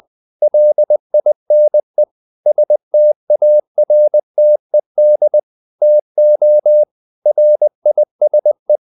Morse code, heard as a keyed tone, about 20 wpm, 600 Hz. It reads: LINE STARTED TO RISE
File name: the old man and the sea / the old man and the sea part 1 20wpm